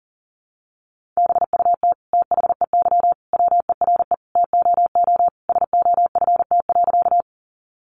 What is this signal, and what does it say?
Morse code, heard as a keyed tone, about 40 wpm, 700 Hz. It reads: BUT THEY WERE TOO SOFT.
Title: Sentences from Top 1500 Words 40wpm